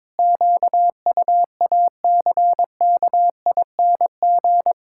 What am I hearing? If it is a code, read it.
Morse code, heard as a keyed tone, about 22 wpm, 700 Hz. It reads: QUACKING
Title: Pangram Challenge v2 - 22wpm